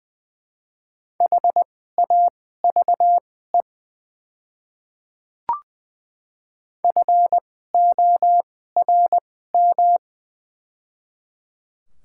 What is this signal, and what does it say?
Morse code, heard as a keyed tone, about 20 wpm, spaced as for 14 wpm, 700 Hz. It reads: HAVE FORM
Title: 2x Character Spacing - Top 200 Words - 20wpm